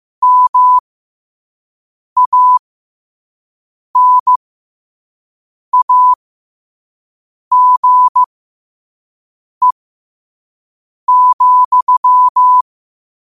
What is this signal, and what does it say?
Morse code, heard as a keyed tone, about 15 wpm, spaced as for 5 wpm, 1000 Hz. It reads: MANAGE,
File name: qotdfiles / 5wpm